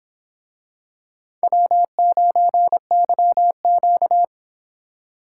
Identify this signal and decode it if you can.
Morse code, heard as a keyed tone, about 24 wpm, 700 Hz. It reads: W9YQ